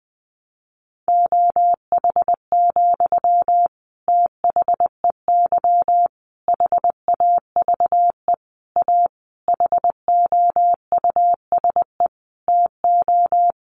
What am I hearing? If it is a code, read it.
Morse code, heard as a keyed tone, about 20 wpm, 700 Hz. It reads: OH, THEY HAVE A HOUSE TO
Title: Sentences from Top 800 Words 20wpm